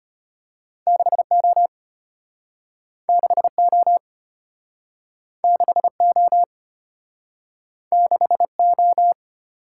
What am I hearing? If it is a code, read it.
Morse code, heard as a keyed tone, about 33 wpm, 700 Hz. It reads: 6O 6O 6O 6O